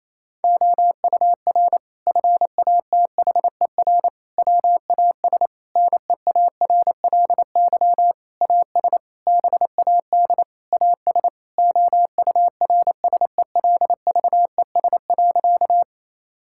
Morse code, heard as a keyed tone, about 28 wpm, 700 Hz. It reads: OUR FATHER WAS NEARLY AS BAD AS OURSELVES.
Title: Sentences from Top 1600 Words 28wpm